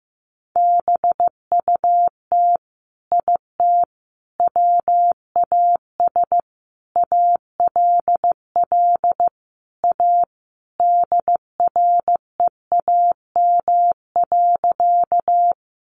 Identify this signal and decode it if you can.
Morse code, heard as a keyed tone, about 15 wpm, 700 Hz. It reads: BUT IT WAS ALL A DREAM.
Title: Sentences from Top 1500 Words 15wpm